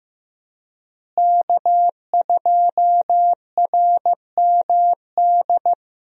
Morse code, heard as a keyed tone, about 15 wpm, 700 Hz. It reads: K2RMD
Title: USA Call Signs - 5 Characters - Repeated 2x - 15wpm